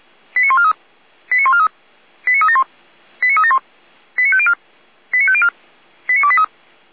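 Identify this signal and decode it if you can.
Five-tone selective call (ZVEI), heard as quick runs of five tones, tones of about 70 ms. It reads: 89143 89143 89371 89371 89504 89504 89283
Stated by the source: Five-tone ZVEI selective call, received by SDR, fire brigade Orlamünde.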